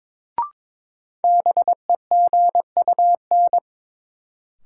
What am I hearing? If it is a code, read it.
Morse code, heard as a keyed tone, about 22 wpm, 700 Hz. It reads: BEGUN
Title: Sentences from Top 1000 Words - Review Part 6 - New Word Drill - 22wpm